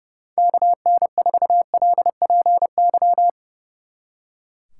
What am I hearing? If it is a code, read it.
Morse code, heard as a keyed tone, about 34 wpm, 700 Hz. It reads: KN4LPY
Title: Speed Racing - USA Call Signs 30wpm